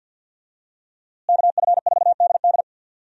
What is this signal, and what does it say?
Morse code, heard as a keyed tone, about 46 wpm, 700 Hz. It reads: KF4DD
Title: USA Call Signs - 5 Characters - Repeated 3x - 50wpm